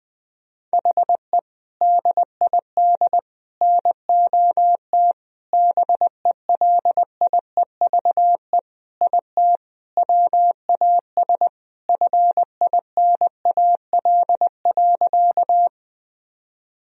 Morse code, heard as a keyed tone, about 20 wpm, 700 Hz. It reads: HE DID NOT BELIEVE IT WAS FINAL.
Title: Sentences from Top 1000 Words - Review Part 2 - 20wpm